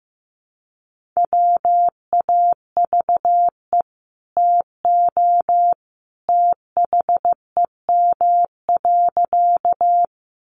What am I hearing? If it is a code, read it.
Morse code, heard as a keyed tone, about 15 wpm, 700 Hz. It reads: WAVE TO THEM.
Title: Sentences from Top 1400 Words 15wpm